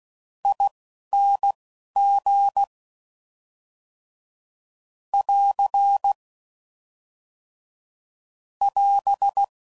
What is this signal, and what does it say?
Morse code, heard as a keyed tone, about 16 wpm, spaced as for 11 wpm, 800 Hz. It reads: ING <AR> <AS>